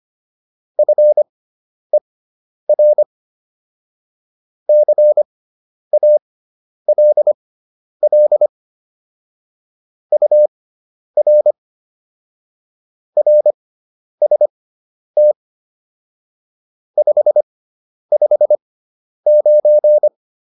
Morse code, heard as a keyed tone, about 25 wpm, spaced as for 10 wpm, 600 Hz. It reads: FER CALL UR RST 559
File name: QSO 410 10